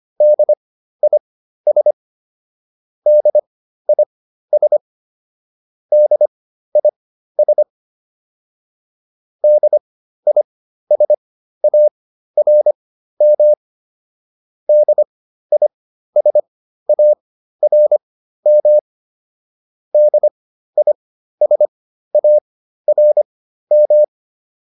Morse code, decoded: DIS DIS DIS DISARM DISARM DISARM

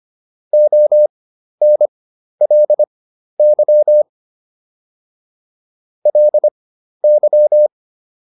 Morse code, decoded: ONLY LY

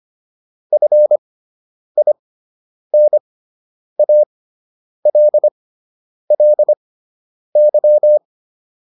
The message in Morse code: FINALLY

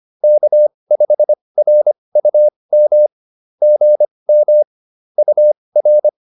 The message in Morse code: K5RUM GM UR

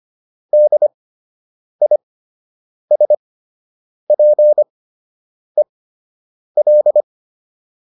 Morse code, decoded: DISPEL

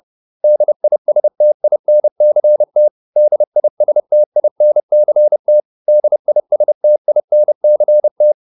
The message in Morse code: DISTINCT DISTINCT DISTINCT

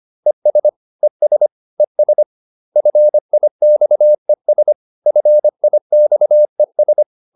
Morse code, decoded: ES ES ES FIXES FIXES

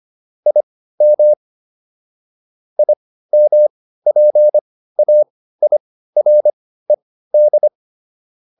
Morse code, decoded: IM IMPAIRED

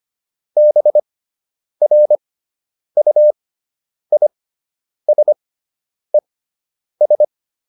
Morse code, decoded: BRUISES

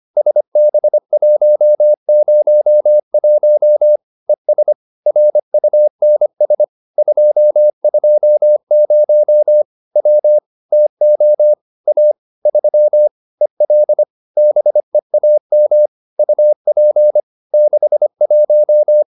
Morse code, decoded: SB101 ES RUNS 220 W TO A 3 EL BEAM UP 61

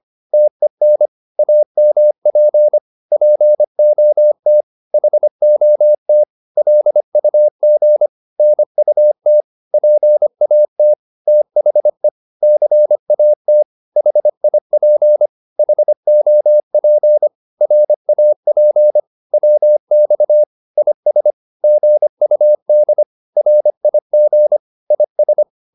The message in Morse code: TEN AMP POT HOT LUG NUT PAT THE CAT HIP HOP RAP WX IS GUD RIG IS

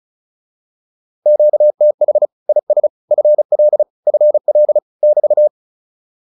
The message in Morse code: QTH IS FL FL <BT>

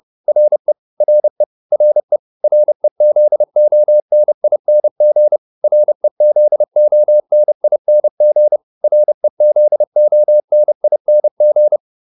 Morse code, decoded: RE RE RE REZONING REZONING REZONING

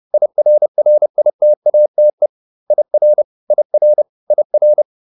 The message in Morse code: IRRITATE IR IR IR

